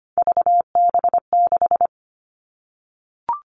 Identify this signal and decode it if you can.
Morse code, heard as a keyed tone, about 25 wpm, 700 Hz. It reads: VB6
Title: All Cut Numbers - 3-Digit 25wpm